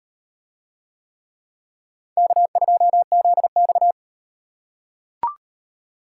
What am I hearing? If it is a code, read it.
Morse code, heard as a keyed tone, about 36 wpm, 700 Hz. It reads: K2ZX